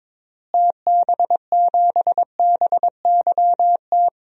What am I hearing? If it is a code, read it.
Morse code, heard as a keyed tone, about 22 wpm, 700 Hz. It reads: TB7BYT
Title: DXCC-100 Call Signs - 6 Characters - Repeated 3x - 22wpm